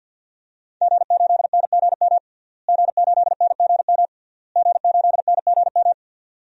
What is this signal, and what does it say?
Morse code, heard as a keyed tone, about 52 wpm, 700 Hz. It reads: G8NGM G8NGM G8NGM
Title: DXCC-100 Call Signs - 5 Characters - Repeated 3x - 50wpm